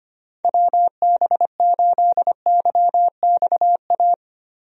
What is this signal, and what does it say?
Morse code, heard as a keyed tone, about 25 wpm, 700 Hz. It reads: WB8YXA
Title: USA Call Signs - 6 Characters - Repeated 2x - 25wpm